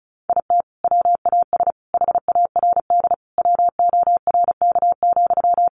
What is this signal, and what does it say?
Morse code, heard as a keyed tone, about 34 wpm, 700 Hz. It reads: IT WAS HARD WORK,
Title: Sentences from Top 600 Words 35wpm